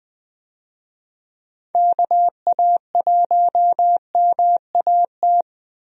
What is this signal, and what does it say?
Morse code, heard as a keyed tone, about 20 wpm, 700 Hz. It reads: KA1MAT
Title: USA Call Signs - 6 Characters - Repeated 3x - 20wpm